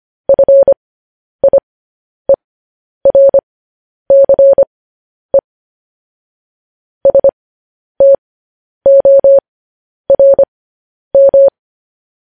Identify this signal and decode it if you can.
Morse code, heard as a keyed tone, about 25 wpm, spaced as for 10 wpm, 550 Hz. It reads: FIERCE STORM